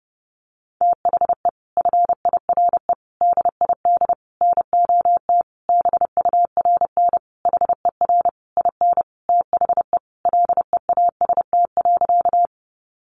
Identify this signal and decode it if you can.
Morse code, heard as a keyed tone, about 30 wpm, 700 Hz. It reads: THE FIRE DID NOT BURN HER IN THE LEAST.
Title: Sentences from Top 1300 Words 30wpm